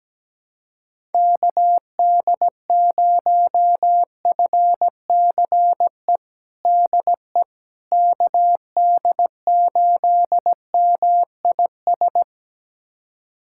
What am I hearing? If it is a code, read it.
Morse code, heard as a keyed tone, about 17 wpm, 700 Hz. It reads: KD0FCE DE KD8MIS
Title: USA Callsign DE Callsign 17wpm